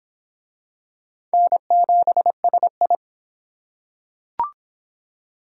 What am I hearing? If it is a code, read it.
Morse code, heard as a keyed tone, about 25 wpm, 700 Hz. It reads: N7SI